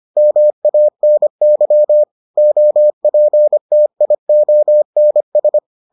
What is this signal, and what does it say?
Morse code, heard as a keyed tone, about 25 wpm, 600 Hz. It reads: MANY OPTIONS